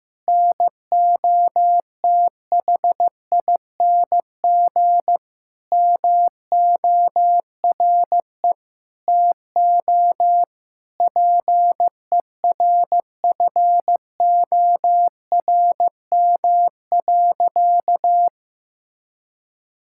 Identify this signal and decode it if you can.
Morse code, heard as a keyed tone, about 16 wpm, 700 Hz. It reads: NOTHING MORE TO PERFORM.
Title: Sentences from Top 1000 Words - Review Part 5 - 15wpm